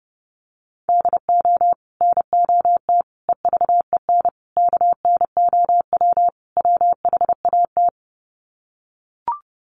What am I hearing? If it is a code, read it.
Morse code, heard as a keyed tone, about 30 wpm, 700 Hz. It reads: DO NOT EVEN KNOW WHAT
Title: Sets of 5 Words Encore 30wpm